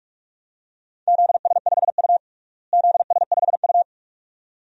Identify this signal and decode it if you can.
Morse code, heard as a keyed tone, about 42 wpm, 700 Hz. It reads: ZS5U ZS5U
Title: DXCC-100 Call Signs - 4 Characters - Repeated 2x - 45wpm